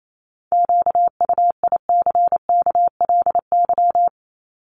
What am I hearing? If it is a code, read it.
Morse code, heard as a keyed tone, about 28 wpm, 700 Hz. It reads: QUICKLY
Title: Top 501-750 Words 28wpm